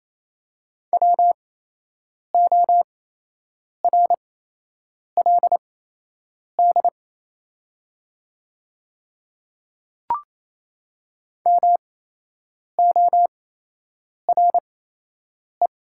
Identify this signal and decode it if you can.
Morse code, heard as a keyed tone, about 28 wpm, spaced as for 8 wpm, 700 Hz. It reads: WORLD MORE
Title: 8x Character Spacing - Top 500 Words - 28wpm